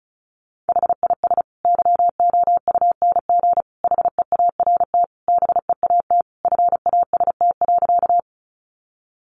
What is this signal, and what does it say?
Morse code, heard as a keyed tone, about 34 wpm, 700 Hz. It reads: HIS YOUNG HEART BEAT FAST.
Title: Sentences from Top 1300 Words 35wpm